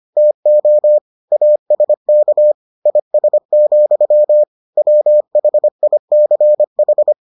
Morse code, heard as a keyed tone, about 25 wpm, 600 Hz. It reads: TO ASK IS, WHICH